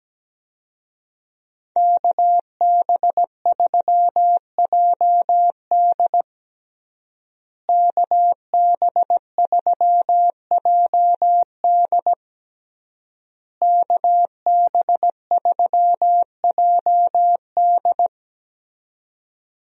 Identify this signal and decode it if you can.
Morse code, heard as a keyed tone, about 17 wpm, 700 Hz. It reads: KB3JD KB3JD KB3JD
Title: USA Call Signs - 5 Characters - Repeated 3x - 17wpm